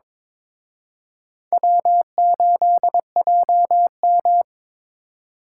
Morse code, heard as a keyed tone, about 26 wpm, 700 Hz. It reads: W8JM
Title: Speed Racing - USA 1x2 Call Signs 22wpm